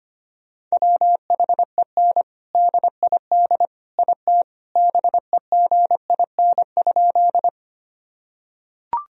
Morse code, heard as a keyed tone, about 25 wpm, 700 Hz. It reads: WHEN DID IT BEGIN?
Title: Sentences from Top 1000 Words - Review Part 1 - 25wpm